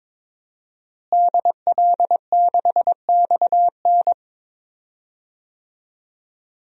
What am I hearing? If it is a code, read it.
Morse code, heard as a keyed tone, about 22 wpm, 700 Hz. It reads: DL6XN
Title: DXCC-100 Call Signs - 5 Characters - Repeated 3x - 22wpm